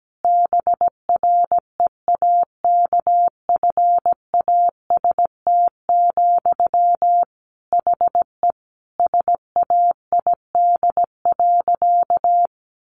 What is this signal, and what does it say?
Morse code, heard as a keyed tone, about 17 wpm, 700 Hz. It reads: BREAKFAST, HE SAID.
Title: Sentences from Top 1500 Words 17wpm